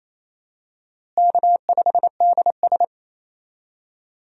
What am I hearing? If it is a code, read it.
Morse code, heard as a keyed tone, about 28 wpm, 700 Hz. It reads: K5DS
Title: USA Call Signs - 4 Characters - Repeated 3x - 28wpm